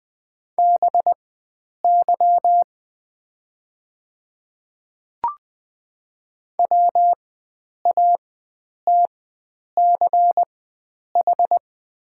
Morse code, heard as a keyed tone, about 20 wpm, spaced as for 9 wpm, 700 Hz. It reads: BY WATCH
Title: 4x Character Spacing - Top 300 Words - 20wpm